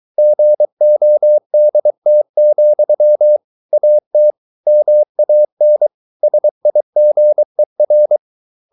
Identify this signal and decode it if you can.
Morse code, heard as a keyed tone, about 23 wpm, 600 Hz. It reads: GODT, AT MAN SIGER
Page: Ferie 23 wpm